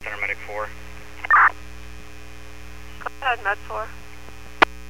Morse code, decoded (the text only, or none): none